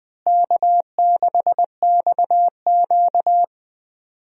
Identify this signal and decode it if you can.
Morse code, heard as a keyed tone, about 20 wpm, 700 Hz. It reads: K6XQ
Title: USA Call Signs - 4 Characters - Repeated 3x - 20wpm